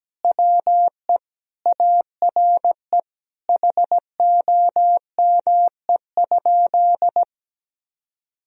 Morse code, heard as a keyed tone, about 17 wpm, 700 Hz. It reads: WE ARE HOME?
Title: Sentences from Top 1000 Words - Review Part 1 - 17wpm